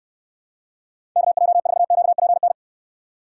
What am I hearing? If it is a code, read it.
Morse code, heard as a keyed tone, about 69 wpm, 700 Hz. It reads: KJ4QPN